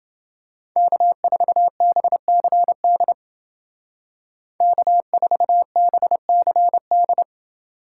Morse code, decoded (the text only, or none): K4BCD K4BCD